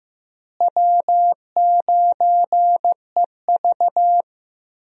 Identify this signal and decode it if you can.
Morse code, heard as a keyed tone, about 15 wpm, 700 Hz. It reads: W9EV
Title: USA Call Signs - 4 Characters - Repeated 2x - 15wpm